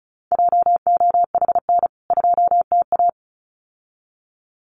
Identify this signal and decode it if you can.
Morse code, heard as a keyed tone, about 34 wpm, 700 Hz. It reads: JOHN 2TA
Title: CWT Exchange 35wpm